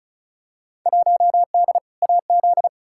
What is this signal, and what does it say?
Morse code, heard as a keyed tone, about 36 wpm, 700 Hz. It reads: 1D AZ
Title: Field Day Exchanges - 35wpm